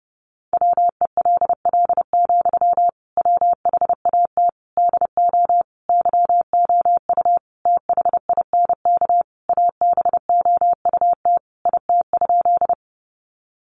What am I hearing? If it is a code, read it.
Morse code, heard as a keyed tone, about 30 wpm, 700 Hz. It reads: WELL, WHAT DO YOU THINK ABOUT IT?
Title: Sentences from Top 200 Words 30wpm